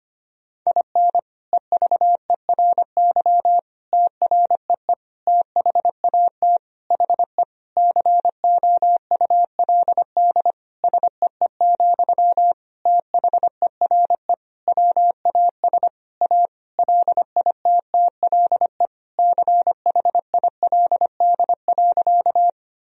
Morse code, heard as a keyed tone, about 25 wpm, 700 Hz. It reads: IN EVERY TREE THAT HE COULD SEE, THERE WAS A LITTLE CHILD.
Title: Sentences from Top 1000 Words - Review Part 1 - 25wpm